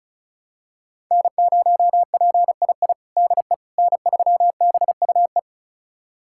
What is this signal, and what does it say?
Morse code, heard as a keyed tone, about 34 wpm, 700 Hz. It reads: N0PII DE N3BUE
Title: USA Callsign DE Callsign - 5 Characters - 35wpm